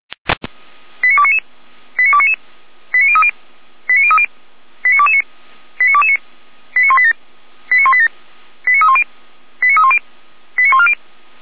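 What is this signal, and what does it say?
Five-tone selective call (ZVEI), heard as quick runs of five tones, tones of about 70 ms. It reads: 89299 89299 89030 89030 89229 89229 89187 89187 89310 89310 89140
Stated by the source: ZVEI five-tone selective call received by SDR, Feuerwehr Orlamünde.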